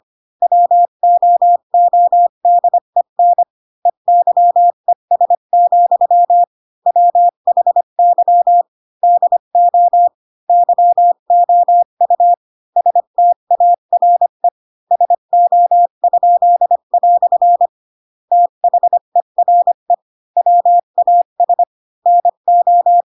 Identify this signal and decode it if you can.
Morse code, heard as a keyed tone, about 25 wpm, 700 Hz. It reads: WOODEN EYES, WHY DO YOU STARE SO?" THERE WAS NO
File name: Pinocchio / Pinocchio ch3 20 25 ik1hge